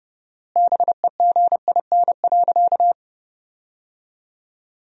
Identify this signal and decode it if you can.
Morse code, heard as a keyed tone, about 30 wpm, 700 Hz. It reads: BEGIN.
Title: 4x Word-Spacing - Sentences from Top 300 Words - 30wpm